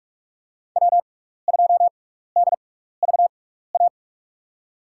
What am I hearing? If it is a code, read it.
Morse code, heard as a keyed tone, about 44 wpm, spaced as for 15 wpm, 700 Hz. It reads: W2DVA
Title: USA Call Signs - 5 Characters - 6x Character Spacing - 45wpm